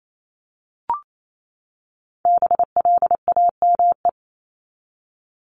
Morse code, heard as a keyed tone, about 28 wpm, 700 Hz. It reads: BLAME
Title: Top 1751-2000 Words 28wpm